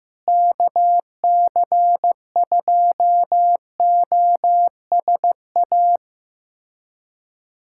Morse code, heard as a keyed tone, about 15 wpm, 700 Hz. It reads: KC2OSA
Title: USA Callsign DE Callsign 15wpm